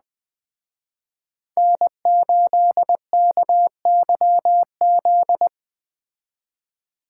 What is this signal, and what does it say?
Morse code, heard as a keyed tone, about 20 wpm, 700 Hz. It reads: N8KYZ